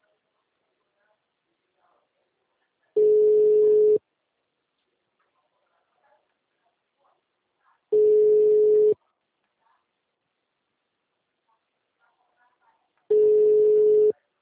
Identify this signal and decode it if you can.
Ringback tone (Europe, 425 Hz), with speech.